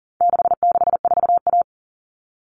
Morse code, heard as a keyed tone, about 40 wpm, 700 Hz. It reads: 664A
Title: All Cut Numbers - 4-Digit 40wpm